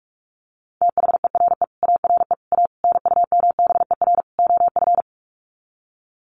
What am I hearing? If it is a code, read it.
Morse code, heard as a keyed tone, about 44 wpm, 700 Hz. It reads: THERE ARE A NUMBER OF
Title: Sets of 5 Words 45wpm